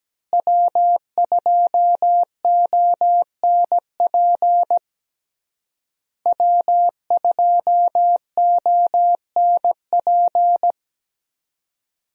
Morse code, heard as a keyed tone, about 17 wpm, 700 Hz. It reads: W2ONP W2ONP
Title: USA Call Signs - 5 Characters - Repeated 3x - 17wpm